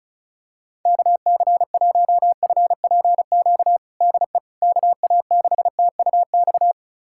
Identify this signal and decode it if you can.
Morse code, heard as a keyed tone, about 34 wpm, 700 Hz. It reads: KC1FPQ DE KA6TUX
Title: USA Callsign DE Callsign 35wpm